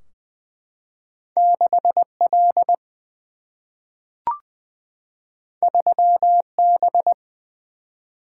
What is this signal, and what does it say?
Morse code, heard as a keyed tone, about 20 wpm, 700 Hz. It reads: 6L 3B
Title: Single-Letter Single-Number 20wpm